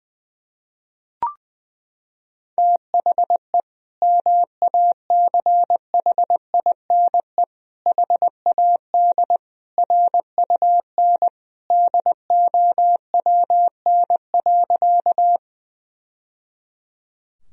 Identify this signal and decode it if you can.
Morse code, heard as a keyed tone, about 20 wpm, 700 Hz. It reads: THE MACHINE HAD RUN DOWN.
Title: Sentences from Top 1000 Words - Review Part 2 - 20wpm